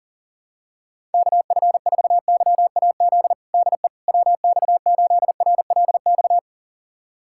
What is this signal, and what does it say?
Morse code, heard as a keyed tone, about 40 wpm, 700 Hz. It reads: KF4YAZ DE WX8RLX
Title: USA Callsign DE Callsign 40wpm